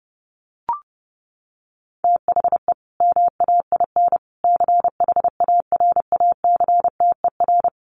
Morse code, heard as a keyed tone, about 30 wpm, 700 Hz. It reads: THE MAIN CHARACTER
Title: Additional Easy Phrases 30wpm